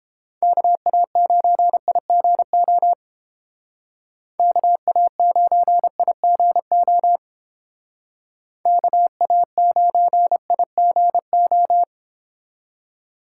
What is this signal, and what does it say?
Morse code, heard as a keyed tone, about 31 wpm, 700 Hz. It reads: KA9IGO KA9IGO KA9IGO